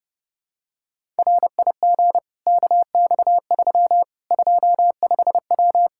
Morse code, heard as a keyed tone, about 32 wpm, 700 Hz. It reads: RIG KX3 25W
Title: Practice QSO - Exchange 2 - Info - 30wpm